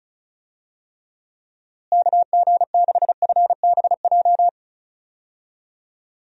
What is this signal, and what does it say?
Morse code, heard as a keyed tone, about 36 wpm, 700 Hz. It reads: KG6FBJ
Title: USA Call Signs - 6 Characters - Repeated 2x - 35wpm